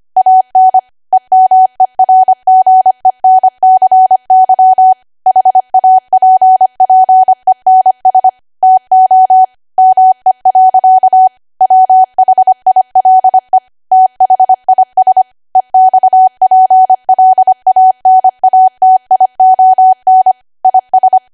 Morse code, heard as a keyed tone, about 25 wpm, 750 Hz. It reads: AN EMERGENCY HAPPENS TO ME. WHILE THIS EXPLANATION IS
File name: Archive / 25 WPM / 25WPM